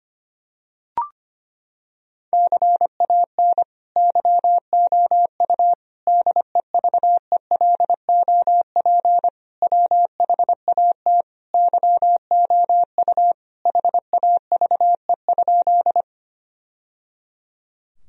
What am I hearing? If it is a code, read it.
Morse code, heard as a keyed tone, about 25 wpm, 700 Hz. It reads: CAN YOU DEVELOP WHAT YOU HAVE?
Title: Sentences from Top 1000 Words - Review Part 2 - 25wpm